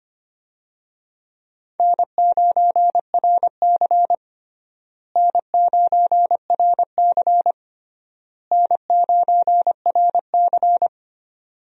Morse code, heard as a keyed tone, about 25 wpm, 700 Hz. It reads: N9RC N9RC N9RC
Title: USA Call Signs - 4 Characters - Repeated 3x - 25wpm